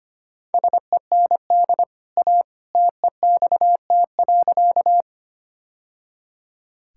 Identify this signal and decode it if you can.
Morse code, heard as a keyed tone, about 25 wpm, 690 Hz. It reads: SEND A TEXT.
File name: Easy Sentences with Hints - 25wpm